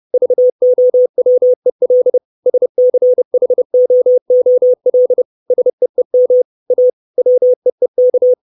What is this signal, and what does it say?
Morse code, heard as a keyed tone, about 30 wpm, 500 Hz. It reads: VOWEL SCHOOL SEEM A WEEK
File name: 632 Words 30WPM